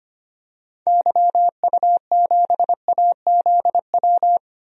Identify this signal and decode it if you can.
Morse code, heard as a keyed tone, about 25 wpm, 700 Hz. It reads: YU7AZW
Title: DXCC-100 Call Signs - 6 Characters - 25wpm